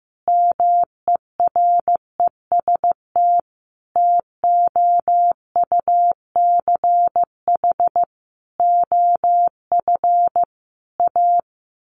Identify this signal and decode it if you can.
Morse code, heard as a keyed tone, about 16 wpm, 700 Hz. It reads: MEREST TOUCH OF A
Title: A Sound Of Thunder 15wpm